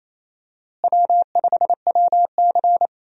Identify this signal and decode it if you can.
Morse code, heard as a keyed tone, about 28 wpm, 700 Hz. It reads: W5WC
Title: USA Call Signs - 4 Characters - Repeated 3x - 28wpm